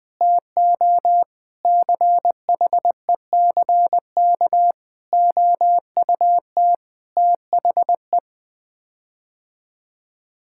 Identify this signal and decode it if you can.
Morse code, heard as a keyed tone, about 20 wpm, 700 Hz. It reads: TO CHECK OUT THE
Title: Sets of 6 Words - 20wpm